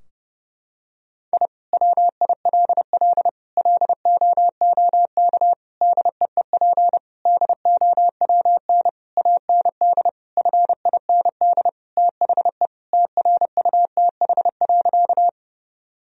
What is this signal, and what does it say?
Morse code, heard as a keyed tone, about 32 wpm, 700 Hz. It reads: I WILL LOOK DEEP DOWN AND FIND THE TRUTH.
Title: Sentences from Top 1000 Words - Review Part 3 - 30wpm